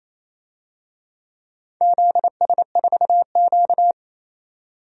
Morse code, heard as a keyed tone, about 29 wpm, 700 Hz. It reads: ZS4Q
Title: DXCC-100 Call Signs - 4 Characters - 28wpm